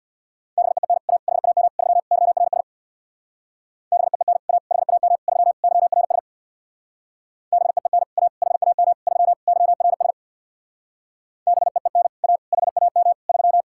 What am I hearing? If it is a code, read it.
Morse code, heard as a keyed tone, about 62 wpm, 700 Hz. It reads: BEEN A HAM 3 YRS BEEN A HAM 3 YRS BEEN A HAM 3 YRS BEEN A HAM 3
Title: Speed-Racing Common QSO Phrases - 50wpm